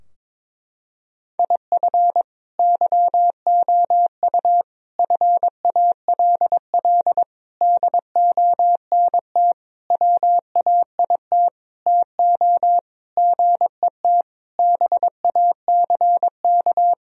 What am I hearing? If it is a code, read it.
Morse code, heard as a keyed tone, about 22 wpm, 700 Hz. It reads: IF YOU FALL DONT WAIT TO GET BACK
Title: Sentences from Top 1000 Words - Review Part 2 - 22wpm